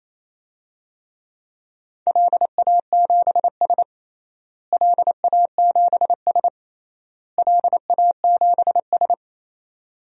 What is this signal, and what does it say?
Morse code, heard as a keyed tone, about 28 wpm, 700 Hz. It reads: LA7S LA7S LA7S
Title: DXCC-100 Call Signs - 4 Characters - Repeated 3x - 28wpm